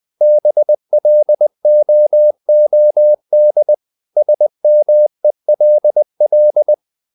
Morse code, heard as a keyed tone, about 20 wpm, 600 Hz. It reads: BLOOD SMELL
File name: the old man and the sea part 2 20wpm